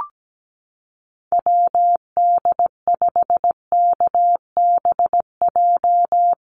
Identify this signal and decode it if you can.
Morse code, heard as a keyed tone, about 17 wpm, 700 Hz. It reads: WD5KBJ
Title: USA Call Signs Repeated 2x 17wpm